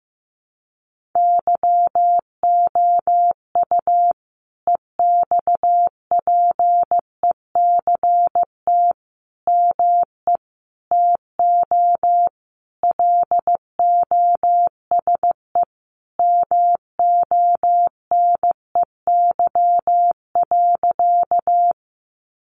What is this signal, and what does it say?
Morse code, heard as a keyed tone, about 15 wpm, 700 Hz. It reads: YOU EXPECT ME TO LOSE MONEY.
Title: Sentences from Top 1100 Words 15wpm